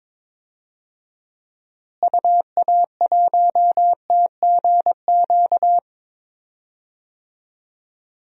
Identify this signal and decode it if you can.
Morse code, heard as a keyed tone, about 22 wpm, 700 Hz. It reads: UA1TGQ
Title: DXCC-100 Call Signs - 6 Characters - Repeated 2x - 22wpm